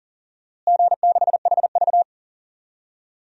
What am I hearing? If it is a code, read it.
Morse code, heard as a keyed tone, about 40 wpm, 700 Hz. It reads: G6HV
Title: DXCC-100 Call Signs - 4 Characters - Repeated 3x - 40wpm